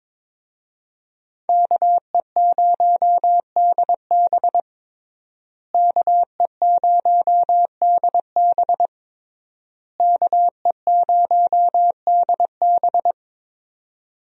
Morse code, heard as a keyed tone, about 22 wpm, 700 Hz. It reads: KE0DB KE0DB KE0DB
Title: USA Call Signs - 5 Characters - Repeated 3x - 22wpm